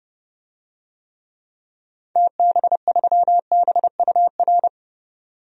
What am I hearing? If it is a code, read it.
Morse code, heard as a keyed tone, about 30 wpm, 700 Hz. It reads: TB3BUR
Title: DXCC-100 Call Signs - 6 Characters - 30wpm